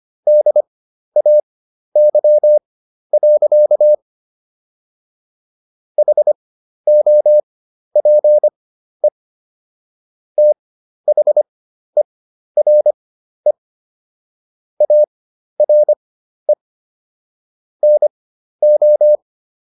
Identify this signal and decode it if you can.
Morse code, heard as a keyed tone, about 25 wpm, spaced as for 12 wpm, 600 Hz. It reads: DAY. HOPE THERE ARE NO
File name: SS017 12